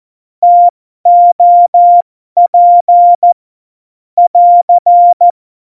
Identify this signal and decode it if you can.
Morse code, heard as a keyed tone, about 14 wpm, spaced as for 12 wpm, 700 Hz. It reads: TOP <AR>
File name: Morsekurs / Witze